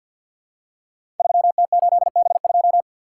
Morse code, heard as a keyed tone, about 48 wpm, 700 Hz. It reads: 3T8B2
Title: Code Groups - 5 Letter:Numbers - 50wpm